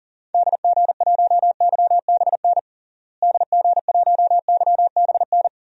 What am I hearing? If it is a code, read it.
Morse code, heard as a keyed tone, about 40 wpm, 700 Hz. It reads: DG1YBN DG1YBN
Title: DXCC-100 Call Signs - 6 Characters - Repeated 2x - 40wpm